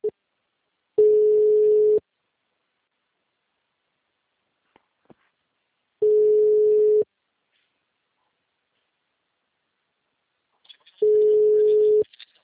Ringback tone (Europe, 425 Hz).
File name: out- -2021- - - .5928